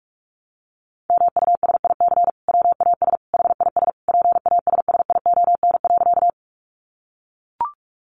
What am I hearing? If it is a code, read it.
Morse code, heard as a keyed tone, about 44 wpm, 700 Hz. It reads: MUSIC WAS HIS PASSION.
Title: Sentences from Top 1300 Words 45wpm